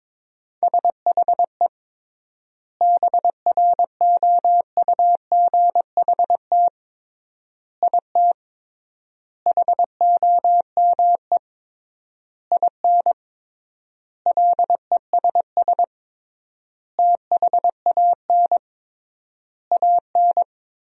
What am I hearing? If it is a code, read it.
Morse code, heard as a keyed tone, about 22 wpm, 700 Hz. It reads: SHE BROUGHT IT HOME IN LESS THAN AN
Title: 3x Word-Spacing - Sentences from Top 500 Words - 22wpm